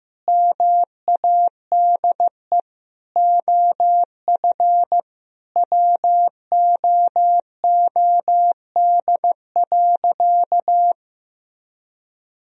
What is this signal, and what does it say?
Morse code, heard as a keyed tone, about 16 wpm, 700 Hz. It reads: MADE OF WOOD.
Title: Sentences from Top 1000 Words - Review Part 3 - 15wpm